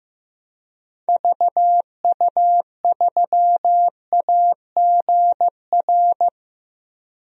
Morse code, heard as a keyed tone, about 15 wpm, 700 Hz. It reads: VU3AGR
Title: DXCC-100 Call Signs - 6 Characters - 15wpm